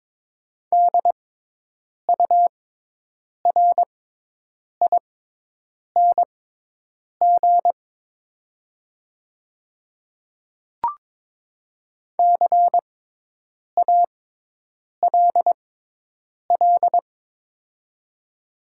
Morse code, heard as a keyed tone, about 22 wpm, spaced as for 8 wpm, 690 Hz. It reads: DURING CALL
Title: 6x Character Spacing - Top 500 Words - 22wpm